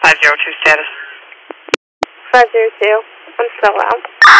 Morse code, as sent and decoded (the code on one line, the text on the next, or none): none